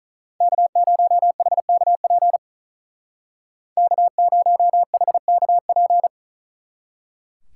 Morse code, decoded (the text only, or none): K0HKP K0HKP